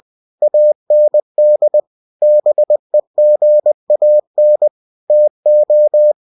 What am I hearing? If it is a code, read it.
Morse code, heard as a keyed tone, about 20 wpm, 600 Hz. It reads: AND BEGAN TO
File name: cw-books / winnie-the-pooh / winnie-20wpm-page-0084